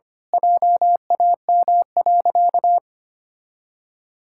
Morse code, heard as a keyed tone, about 25 wpm, 700 Hz. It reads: JAM.
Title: Pangram Challenge v2 - 25wpm